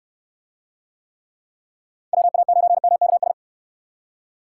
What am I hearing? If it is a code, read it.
Morse code, heard as a keyed tone, about 68 wpm, 700 Hz. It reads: WA9MKS